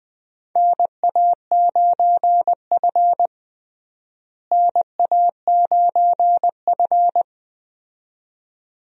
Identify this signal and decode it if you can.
Morse code, heard as a keyed tone, about 20 wpm, 700 Hz. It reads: NA9F NA9F
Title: USA Call Signs - 4 Characters - Repeated 2x - 20wpm